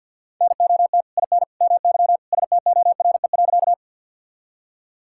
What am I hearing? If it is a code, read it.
Morse code, heard as a keyed tone, about 50 wpm, 700 Hz. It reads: NOT IN MY STORE.
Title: Sentences from Top 1000 Words - Review Part 5 - 50wpm